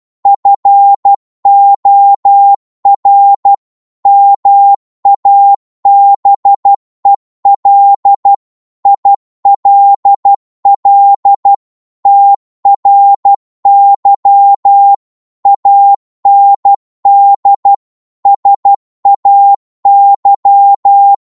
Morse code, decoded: FOR MABEL ILL TRY AND SAY